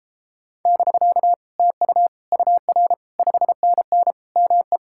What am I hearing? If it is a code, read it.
Morse code, heard as a keyed tone, about 33 wpm, 700 Hz. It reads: <BK> TU UR 5NN ME